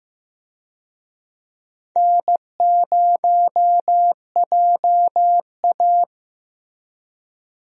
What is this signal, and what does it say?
Morse code, heard as a keyed tone, about 15 wpm, 700 Hz. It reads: N0JA
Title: USA Call Signs - 4 Characters - Repeated 2x - 15wpm